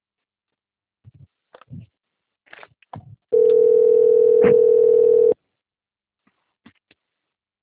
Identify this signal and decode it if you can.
Ringback tone (North America), 440 + 480 Hz.